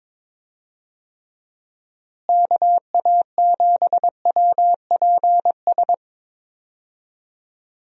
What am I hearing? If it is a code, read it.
Morse code, heard as a keyed tone, about 22 wpm, 700 Hz. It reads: KA7WPS
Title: USA Call Signs - 6 Characters - Repeated 2x - 22wpm